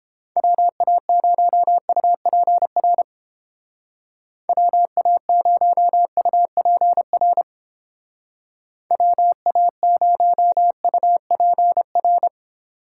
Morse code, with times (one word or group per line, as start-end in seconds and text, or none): 0.4-3.0 WA0UPR
4.5-7.4 WA0UPR
8.9-12.3 WA0UPR